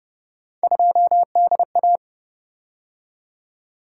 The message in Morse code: 2DA

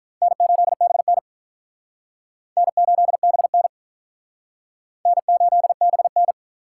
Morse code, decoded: N8BN N8BN N8BN

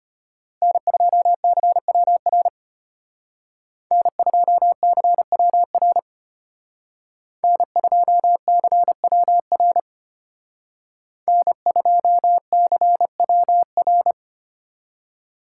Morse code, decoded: N2CWR N2CWR N2CWR N2CWR

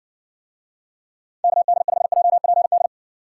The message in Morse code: KD5JPD

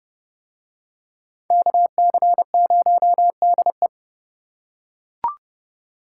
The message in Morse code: KC0DE

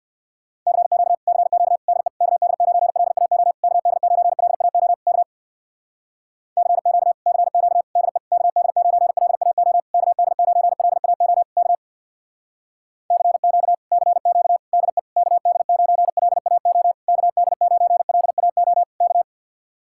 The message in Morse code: CQ CQ DE KD9LAO KD9LAO K CQ CQ DE KD9LAO KD9LAO K CQ CQ DE KD9LAO KD9LAO K

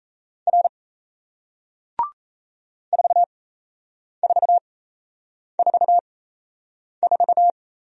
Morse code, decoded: R 4444

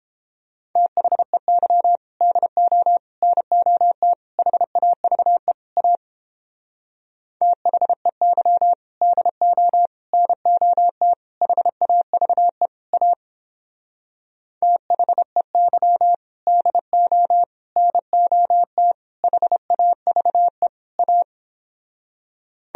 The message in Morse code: THEY DO NOT HAVE A THEY DO NOT HAVE A THEY DO NOT HAVE A

